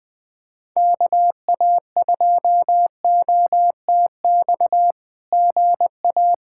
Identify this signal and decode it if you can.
Morse code, heard as a keyed tone, about 19 wpm, 700 Hz. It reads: KA2OTX GA
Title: SOTA QSO Practice 20wpm